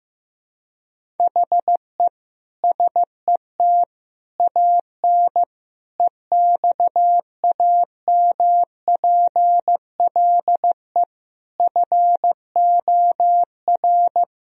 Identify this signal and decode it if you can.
Morse code, heard as a keyed tone, about 16 wpm, 700 Hz. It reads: HE SET AN EXAMPLE FOR
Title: Sentences from Top 1000 Words - Review Part 1 - 15wpm